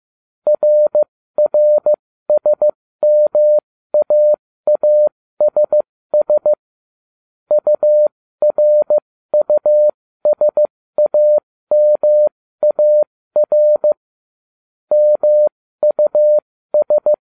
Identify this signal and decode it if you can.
Morse code, heard as a keyed tone, about 15 wpm, spaced as for 13 wpm, 600 Hz. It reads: RRSMAASS URUSAMAR MUS